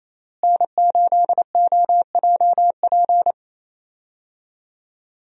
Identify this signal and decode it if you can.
Morse code, heard as a keyed tone, about 28 wpm, 700 Hz. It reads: N8OJP
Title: USA Call Signs - 5 Characters - Repeated 3x - 28wpm